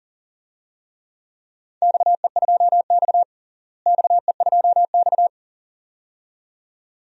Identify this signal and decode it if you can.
Morse code, heard as a keyed tone, about 40 wpm, 700 Hz. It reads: XE2X XE2X
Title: DXCC-100 Call Signs - 4 Characters - Repeated 2x - 40wpm